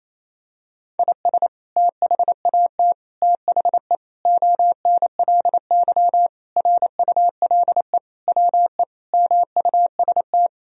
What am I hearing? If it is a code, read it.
Morse code, heard as a keyed tone, about 27 wpm, 700 Hz. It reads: IS THAT THE ONLY RULE WE MUST
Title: Sentences from Top 1000 Words - Review Part 2 - 28wpm